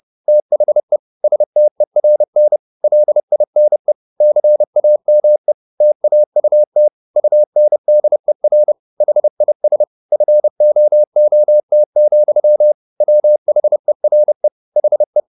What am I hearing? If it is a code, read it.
Morse code, heard as a keyed tone, about 30 wpm, 600 Hz. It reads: THE STERN LINE CAME TAUT UNDER HIS FOOT, WHERE HE
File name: the old man and the sea / the old man and the sea 1 30wpm